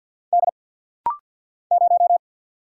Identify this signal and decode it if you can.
Morse code, heard as a keyed tone, about 50 wpm, 690 Hz. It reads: D 0
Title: Single Letter-Number Alt - Rapid-Fire 50wpm